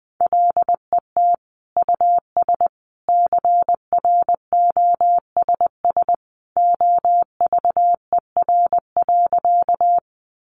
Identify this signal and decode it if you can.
Morse code, heard as a keyed tone, about 20 wpm, 700 Hz. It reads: LET US CROSS OVER.
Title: Sentences from Top 300 Words 20wpm